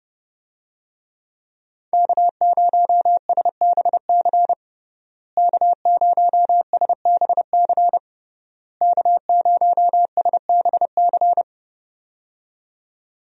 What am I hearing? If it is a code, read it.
Morse code, heard as a keyed tone, about 30 wpm, 700 Hz. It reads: K0SBC K0SBC K0SBC
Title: USA Call Signs - 5 Characters - Repeated 3x - 30wpm